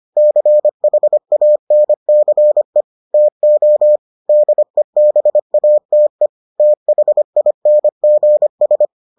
Morse code, heard as a keyed tone, about 25 wpm, 600 Hz. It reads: CHANCE TO DEBATE THINGS